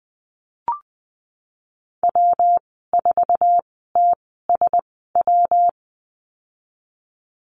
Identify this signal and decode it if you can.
Morse code, heard as a keyed tone, about 20 wpm, spaced as for 14 wpm, 700 Hz. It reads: W4TSW
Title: 173 Lesson 28g - Number 9 - Call Signs 20wpm